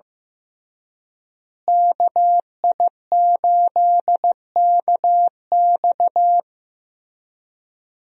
Morse code, heard as a keyed tone, about 15 wpm, 700 Hz. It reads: KI8KX
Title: USA Call Signs - 5 Characters - Repeated 3x - 15wpm